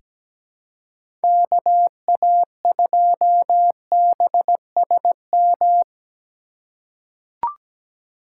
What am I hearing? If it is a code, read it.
Morse code, heard as a keyed tone, about 17 wpm, 700 Hz. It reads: KA2BSM
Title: USA Call Signs - 6 Characters - 3x Character Spacing - 17wpm